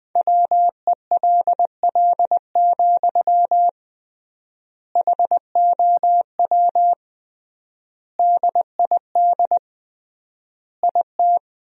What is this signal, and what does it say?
Morse code, heard as a keyed tone, about 21 wpm, 700 Hz. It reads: WELL, HOW DID IT
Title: 3x Word-Spacing - Sentences from Top 200 Words - 20wpm